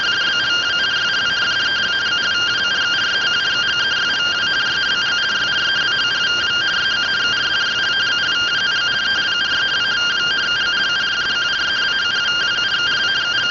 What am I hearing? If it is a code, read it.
Two-tone teleprinter signal, about 170 Hz shift.